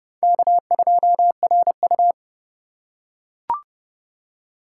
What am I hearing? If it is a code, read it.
Morse code, heard as a keyed tone, about 29 wpm, 700 Hz. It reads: K2RU